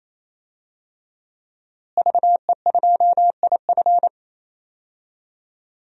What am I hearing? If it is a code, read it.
Morse code, heard as a keyed tone, about 28 wpm, 700 Hz. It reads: VE2IF
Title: DXCC-100 Call Signs - 5 Characters - Repeated 3x - 28wpm